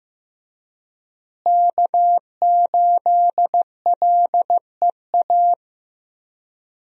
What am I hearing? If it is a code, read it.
Morse code, heard as a keyed tone, about 15 wpm, 700 Hz. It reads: K8LEA
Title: USA Call Signs - 5 Characters - Repeated 3x - 15wpm